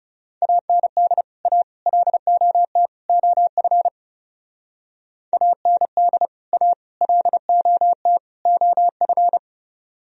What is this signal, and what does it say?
Morse code, heard as a keyed tone, about 40 wpm, 700 Hz. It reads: AND A LOT OF AND A LOT OF